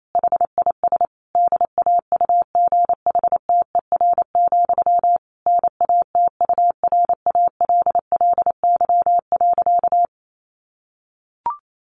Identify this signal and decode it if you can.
Morse code, heard as a keyed tone, about 28 wpm, 700 Hz. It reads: HIS DAUGHTER, NATURALLY.
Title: Sentences from Top 1500 Words 28wpm